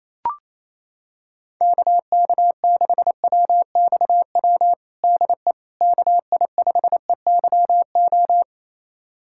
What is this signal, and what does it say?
Morse code, heard as a keyed tone, about 28 wpm, 700 Hz. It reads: KK6WXW DE KI5EYO
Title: USA Callsign DE Callsign - 6 Characters - 28wpm